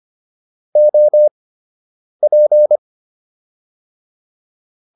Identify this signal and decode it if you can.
Morse code, heard as a keyed tone, about 25 wpm, spaced as for 8 wpm, 600 Hz. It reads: OP